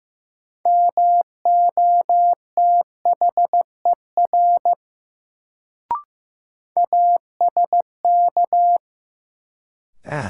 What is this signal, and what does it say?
Morse code, heard as a keyed tone, about 15 wpm, 700 Hz. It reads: MOTHER ASK
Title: Sentences from Top 200 Words - New Word Drill - 15wpm